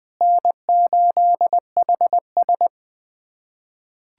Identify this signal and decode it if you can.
Morse code, heard as a keyed tone, about 21 wpm, 700 Hz. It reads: N8HS
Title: USA Callsign DE Callsign - 4 Characters - 20wpm